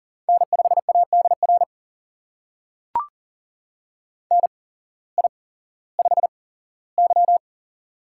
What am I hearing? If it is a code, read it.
Morse code, heard as a keyed tone, about 40 wpm, spaced as for 19 wpm, 700 Hz. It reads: N5ADR NI5Y
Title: USA Call Signs - 8x Character Spacing - 40wpm